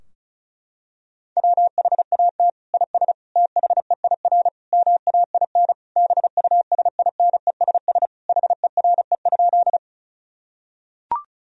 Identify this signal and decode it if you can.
Morse code, heard as a keyed tone, about 35 wpm, 700 Hz. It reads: WHAT IS THEIR MAIN BUSINESS HERE?
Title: Sentences from Top 1000 Words - Review Part 2 - 35wpm